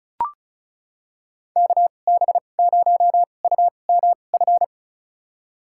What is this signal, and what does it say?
Morse code, heard as a keyed tone, about 34 wpm, spaced as for 25 wpm, 700 Hz. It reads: KB0UMF
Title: USA Call Signs - 6 Characters - 2x Character Spacing - 35wpm